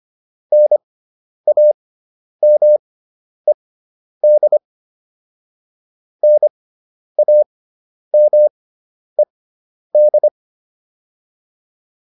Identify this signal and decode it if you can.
Morse code, heard as a keyed tone, about 25 wpm, spaced as for 10 wpm, 600 Hz. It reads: NAMED NAMED